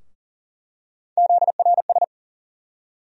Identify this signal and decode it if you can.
Morse code, heard as a keyed tone, about 40 wpm, 700 Hz. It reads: ZRS
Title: USA Call Sign Postfixes - 3 Letters 40wpm